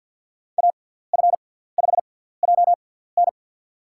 Morse code, heard as a keyed tone, about 48 wpm, spaced as for 17 wpm, 700 Hz. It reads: AF5JN
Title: USA Call Signs - 5 Characters - 6x Character Spacing - 50wpm